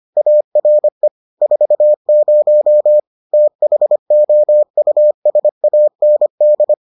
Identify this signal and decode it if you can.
Morse code, heard as a keyed tone, about 25 wpm, 600 Hz. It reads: ARE 40 THOUSAND